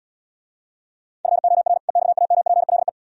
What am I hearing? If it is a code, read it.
Morse code, heard as a keyed tone, about 77 wpm, 700 Hz. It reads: FOR EXAMPLE